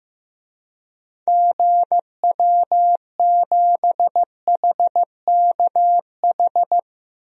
Morse code, heard as a keyed tone, about 15 wpm, 700 Hz. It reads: GW7HKH
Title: DXCC-100 Call Signs - 6 Characters - 15wpm